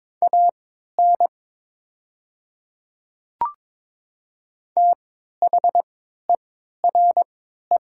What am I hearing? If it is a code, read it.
Morse code, heard as a keyed tone, about 22 wpm, spaced as for 12 wpm, 700 Hz. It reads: AN THERE
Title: 3x Character Spacing - Top 100 Words - 22wpm